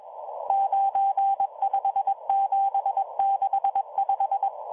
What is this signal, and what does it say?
Morse code, heard as a keyed tone, about 22 wpm, 790 Hz, with radio noise. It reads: 95765